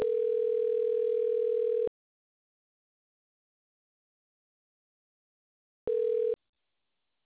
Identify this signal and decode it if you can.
Ringback tone (North America), 440 + 480 Hz.